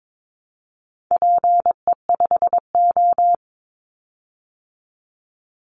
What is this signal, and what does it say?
Morse code, heard as a keyed tone, about 22 wpm, 700 Hz. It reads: PE5O
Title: Call Signs Repeated 3x - 22wpm